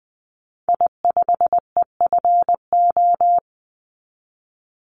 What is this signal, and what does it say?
Morse code, heard as a keyed tone, about 20 wpm, 700 Hz. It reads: I5EFO
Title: Call Signs - 20wpm